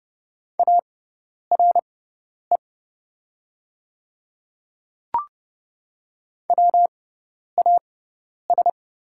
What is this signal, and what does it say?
Morse code, heard as a keyed tone, about 30 wpm, spaced as for 10 wpm, 700 Hz. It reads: ARE WAS
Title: 6x Character Spacing - Top 100 Words - 30wpm